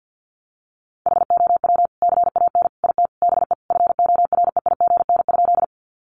Morse code, heard as a keyed tone, about 50 wpm, 700 Hz. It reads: HOW CAN IT BE FOREIGN?
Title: Sentences from Top 1400 Words 50wpm